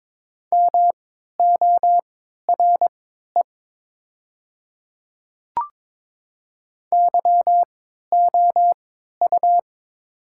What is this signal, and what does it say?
Morse code, heard as a keyed tone, about 22 wpm, spaced as for 12 wpm, 700 Hz. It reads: MORE YOU
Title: 3x Character Spacing - Top 100 Words - 22wpm